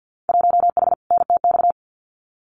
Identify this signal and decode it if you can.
Morse code, heard as a keyed tone, about 50 wpm, 700 Hz. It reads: 1H NTX
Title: Winter Field Day Exchanges 50wpm